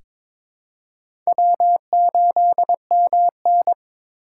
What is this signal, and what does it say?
Morse code, heard as a keyed tone, about 22 wpm, 700 Hz. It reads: W8MN
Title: USA Call Signs - 3x Character Spacing - 22wpm